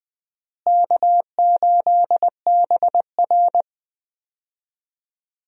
Morse code, heard as a keyed tone, about 21 wpm, 700 Hz. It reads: K8BR
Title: USA Call Signs - 4 Characters - Repeated 3x - 20wpm